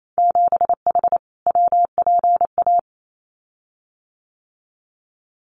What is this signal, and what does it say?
Morse code, heard as a keyed tone, about 28 wpm, 700 Hz. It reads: 7H WPA
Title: Winter Field Day Exchanges Encore 28wpm